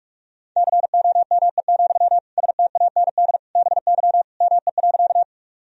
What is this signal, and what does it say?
Morse code, heard as a keyed tone, about 42 wpm, 700 Hz. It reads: COME, STAND BY ME.
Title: Sentences from Top 1000 Words - Review Part 1 - 45wpm